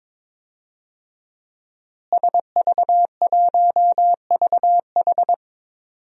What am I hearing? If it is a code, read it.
Morse code, heard as a keyed tone, about 21 wpm, 700 Hz. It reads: SV1VH
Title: DXCC-100 Call Signs - 5 Characters - Repeated 2x - 22wpm